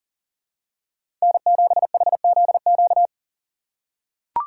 Morse code, heard as a keyed tone, about 40 wpm, 700 Hz. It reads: N7HZQ
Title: USA Call Signs - 5 Characters - 2x Character Spacing - 40wpm